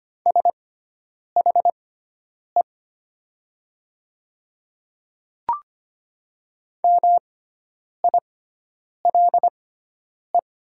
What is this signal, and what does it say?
Morse code, heard as a keyed tone, about 25 wpm, spaced as for 9 wpm, 700 Hz. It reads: SHE MILE
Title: 6x Character Spacing - Top 500 Words - 25wpm